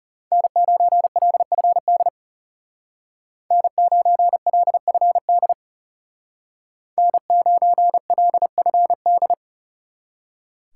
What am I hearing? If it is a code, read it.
Morse code, heard as a keyed tone, about 39 wpm, 700 Hz. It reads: N9LFD N9LFD N9LFD